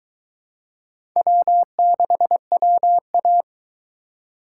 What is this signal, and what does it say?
Morse code, heard as a keyed tone, about 22 wpm, 700 Hz. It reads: W6WA